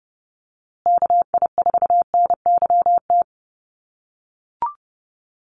Morse code, heard as a keyed tone, about 30 wpm, 700 Hz. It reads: KI4NYT
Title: USA Call Signs Repeated 2x 30wpm